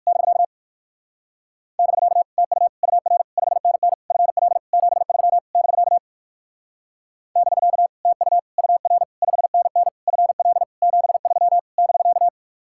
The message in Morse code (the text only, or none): <BK> <BK> TU UR 5NN FL 73 <BK> <BK> TU UR 5NN FL 73 <BK>